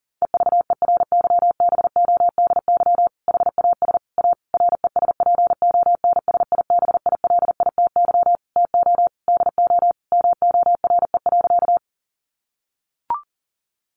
Morse code, heard as a keyed tone, about 40 wpm, 700 Hz. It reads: EVERYBODY HAS A RESPONSIBILITY TO DO MORE.
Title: Sentences from Top 1400 Words 40wpm